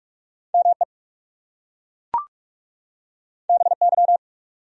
Morse code, heard as a keyed tone, about 38 wpm, 700 Hz. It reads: ME BY